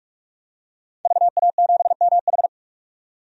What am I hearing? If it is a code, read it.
Morse code, heard as a keyed tone, about 46 wpm, 700 Hz. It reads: VA7MH
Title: DXCC-100 Call Signs - 5 Characters - 45wpm